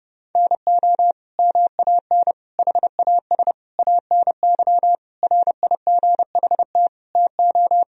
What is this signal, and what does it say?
Morse code, heard as a keyed tone, about 30 wpm, 700 Hz. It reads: NO MAN HAS ANY RIGHT TO
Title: Sentences from Top 1000 Words - Review Part 2 - 30wpm